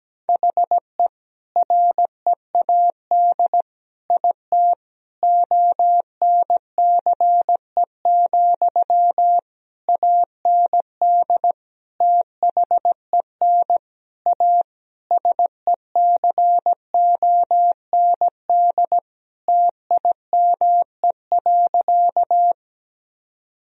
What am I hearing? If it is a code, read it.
Morse code, heard as a keyed tone, about 17 wpm, 700 Hz. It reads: HE READ IT ONCE, AND THEN A SECOND TIME.
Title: Sentences from Top 1000 Words - Review Part 1 - 17wpm